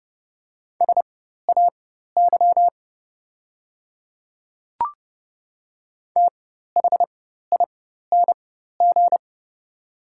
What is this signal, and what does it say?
Morse code, heard as a keyed tone, about 30 wpm, spaced as for 14 wpm, 700 Hz. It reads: SAY THING
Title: 4x Character Spacing - Top 300 Words - 30wpm